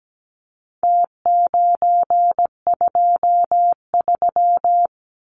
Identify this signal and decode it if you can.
Morse code, heard as a keyed tone, about 17 wpm, 700 Hz. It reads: T923
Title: Numbers - Contest Style - 4-Digit 17wpm